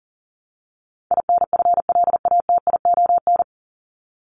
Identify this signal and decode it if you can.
Morse code, heard as a keyed tone, about 40 wpm, 700 Hz. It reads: INFLATION
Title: Top 1001-1250 Words 40wpm